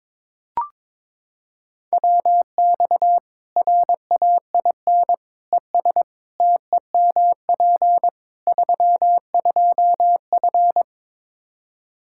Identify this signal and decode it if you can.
Morse code, heard as a keyed tone, about 21 wpm, 700 Hz. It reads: WX RAIN ES TEMP 32F
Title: K1USN SST QSO Practice 22wpm